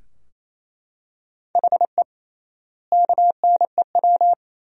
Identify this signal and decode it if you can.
Morse code, heard as a keyed tone, about 28 wpm, 700 Hz. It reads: HE KNEW
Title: 3x Word-Spacing - Sentences from Top 400 Words - 28wpm